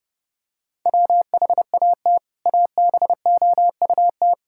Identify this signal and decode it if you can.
Morse code, heard as a keyed tone, about 28 wpm, 700 Hz. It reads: WHAT ABOUT
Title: Sentences from Top 1600 Words 30wpm